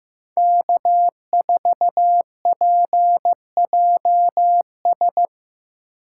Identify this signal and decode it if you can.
Morse code, heard as a keyed tone, about 18 wpm, 700 Hz. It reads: K4PJS
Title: Speed Racing - USA 1x3 Call Signs 15wpm